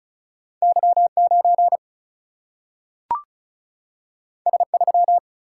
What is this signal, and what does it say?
Morse code, heard as a keyed tone, about 36 wpm, 700 Hz. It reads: Y9 S3
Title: Single-Letter Single-Number 35wpm